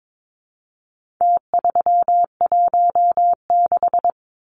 Morse code, heard as a keyed tone, about 22 wpm, 700 Hz. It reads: T316
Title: Numbers - Contest Style - 4-Digit 22wpm